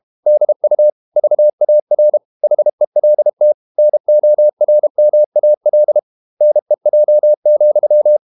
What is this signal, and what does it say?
Morse code, heard as a keyed tone, about 32 wpm, 600 Hz. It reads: DU VAR HELT NORMAL NEJ,
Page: Ferie 32 wpm